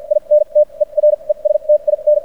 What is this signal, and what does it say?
Morse code, heard as a keyed tone, about 57 wpm, spaced as for 39 wpm, 600 Hz. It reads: INTERESTIN
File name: archived / CRSnet